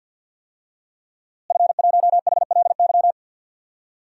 Morse code, heard as a keyed tone, about 50 wpm, 700 Hz. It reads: F1HLY
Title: DXCC-100 Call Signs - 5 Characters - Repeated 3x - 50wpm